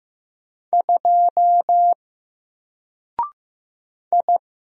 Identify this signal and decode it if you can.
Morse code, heard as a keyed tone, about 15 wpm, 700 Hz. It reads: 2 I
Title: Single Letter-Number 15wpm